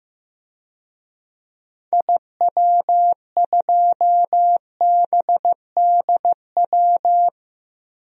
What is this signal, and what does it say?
Morse code, heard as a keyed tone, about 16 wpm, 700 Hz. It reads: IW2BDW
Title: DXCC-100 Call Signs - 6 Characters - Repeated 3x - 15wpm